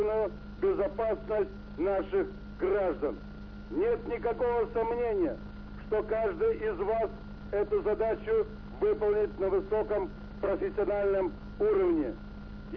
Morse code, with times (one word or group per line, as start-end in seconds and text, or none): none